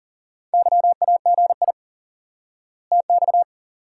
Morse code, heard as a keyed tone, about 40 wpm, 700 Hz. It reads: YAGI TX
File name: IQE Course 3 - Lesson 89 - YAGI - Two Elements - 40wpm